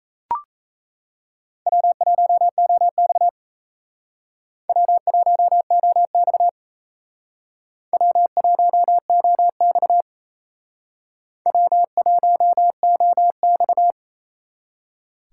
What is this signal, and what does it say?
Morse code, heard as a keyed tone, about 34 wpm, 700 Hz. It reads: W1OX W1OX W1OX W1OX
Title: Speed Racing - USA 1x2 Call Signs 28wpm